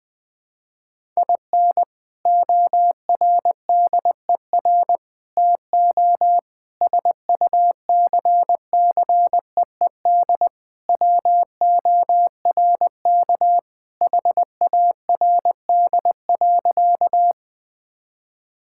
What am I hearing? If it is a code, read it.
Morse code, heard as a keyed tone, about 20 wpm, 700 Hz. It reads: IN ORDER TO SUCCEED WORK HARD.
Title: Sentences with 3-Word Phrases - 20wpm